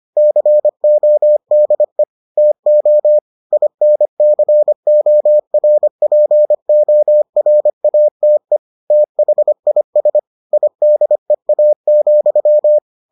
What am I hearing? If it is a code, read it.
Morse code, heard as a keyed tone, about 25 wpm, 600 Hz. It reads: CODE TO INCORPORATE THIS IDEA,